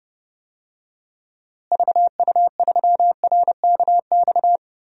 Morse code, heard as a keyed tone, about 30 wpm, 700 Hz. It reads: VU3RKX
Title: DXCC-100 Call Signs - 6 Characters - Repeated 3x - 30wpm